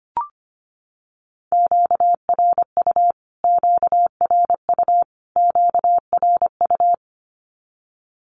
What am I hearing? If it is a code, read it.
Morse code, heard as a keyed tone, about 25 wpm, 700 Hz. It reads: QRU QRU QRU
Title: Q-Codes Repeated 3x 25wpm